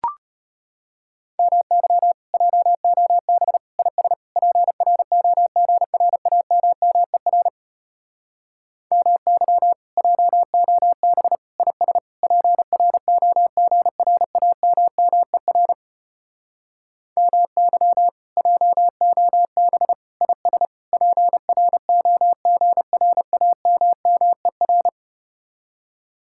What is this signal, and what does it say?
Morse code, heard as a keyed tone, about 33 wpm, 700 Hz. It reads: MY JOB IS PROGRAMMER MY JOB IS PROGRAMMER MY JOB IS PROGRAMMER